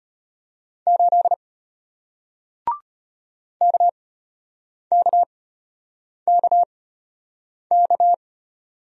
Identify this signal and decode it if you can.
Morse code, heard as a keyed tone, about 34 wpm, spaced as for 8 wpm, 700 Hz. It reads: 8 KKKK